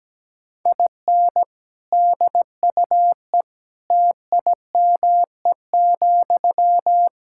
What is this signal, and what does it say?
Morse code, heard as a keyed tone, about 17 wpm, 700 Hz. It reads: IN DUE TIME,
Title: Sentences from Top 1000 Words - Review Part 6 - 17wpm